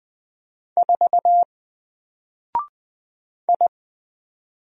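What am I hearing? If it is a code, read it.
Morse code, heard as a keyed tone, about 20 wpm, 700 Hz. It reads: 4 I
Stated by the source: Single letter and number Morse drill at 20 wpm, from N4LCD.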